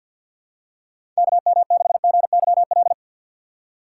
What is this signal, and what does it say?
Morse code, heard as a keyed tone, about 52 wpm, 700 Hz. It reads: KM6GYL
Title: USA Call Signs - 2x Character Spacing - 50wpm